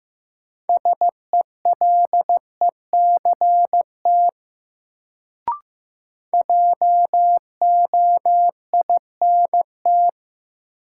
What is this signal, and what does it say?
Morse code, heard as a keyed tone, about 15 wpm, 700 Hz. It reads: SELECT JOINT
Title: Sentences from Top 1300 Words - New Word Drill Alt - 15wpm